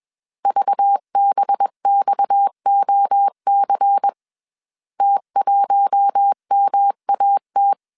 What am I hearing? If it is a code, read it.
Morse code, heard as a keyed tone, about 21 wpm, 800 Hz. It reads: VBXOC T1MAT
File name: CW / DET / 1 6 1500C S A 20wpm 2048